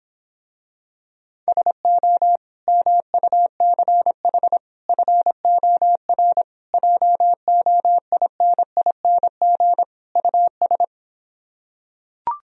Morse code, heard as a keyed tone, about 26 wpm, 700 Hz. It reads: SO MUCH FOR JOINING US